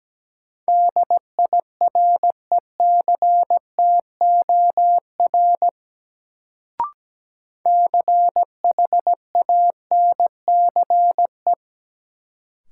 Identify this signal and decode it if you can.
Morse code, heard as a keyed tone, about 17 wpm, 700 Hz. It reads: DIRECTOR CHANCE
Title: Sets of 3 Words - New Word Drill - 17wpm